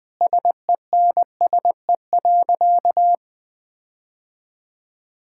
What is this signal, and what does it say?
Morse code, heard as a keyed tone, about 21 wpm, 700 Hz. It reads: SENSE.
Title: Sentences with 5-Word Phrases - 20wpm